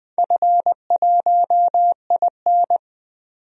Morse code, heard as a keyed tone, about 20 wpm, 700 Hz. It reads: F1IN
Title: DXCC-100 Call Signs - 4 Characters - 20wpm